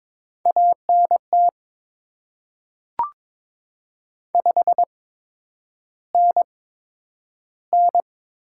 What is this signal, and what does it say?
Morse code, heard as a keyed tone, about 22 wpm, spaced as for 9 wpm, 700 Hz. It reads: ANT 5NN
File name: Top 20 QSO Elements - 8x Character Spacing - 22wpm